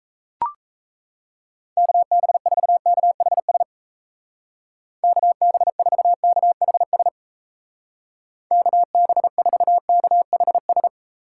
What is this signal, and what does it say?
Morse code, heard as a keyed tone, about 37 wpm, 700 Hz. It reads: KB4KHS KB4KHS KB4KHS